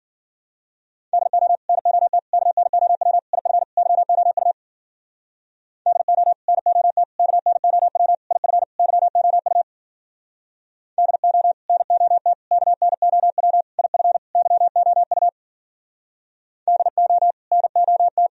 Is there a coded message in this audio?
Yes: Morse code from 1.1 to 18.3 s, heard as a keyed tone, about 55 wpm, 700 Hz. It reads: DO NOT KNOW IF YOU DO NOT KNOW IF YOU DO NOT KNOW IF YOU DO NOT